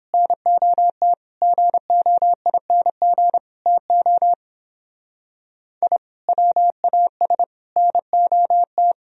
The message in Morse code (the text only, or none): NOT GOING TO I WAS NOT